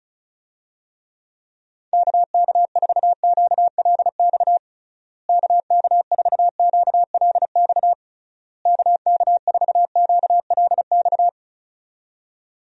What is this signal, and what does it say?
Morse code, heard as a keyed tone, about 36 wpm, 700 Hz. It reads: KK4QLX KK4QLX KK4QLX
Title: USA Call Signs - 6 Characters - Repeated 3x - 35wpm